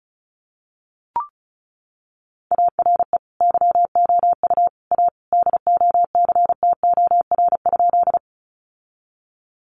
Morse code, heard as a keyed tone, about 34 wpm, 700 Hz. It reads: ARE YOU A DOCTOR?
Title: Sentences from Top 1500 Words 35wpm